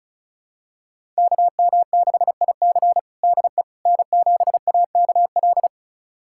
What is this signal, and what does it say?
Morse code, heard as a keyed tone, about 36 wpm, 700 Hz. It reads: KM6IC DE N7AKL
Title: USA Callsign DE Callsign - 5 Characters - 35wpm